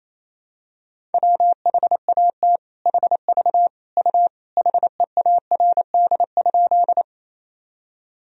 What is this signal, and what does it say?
Morse code, heard as a keyed tone, about 28 wpm, 700 Hz. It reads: WHAT HV U HEARD?
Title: Sentences from Top 400 Words and Abbreviations 28wpm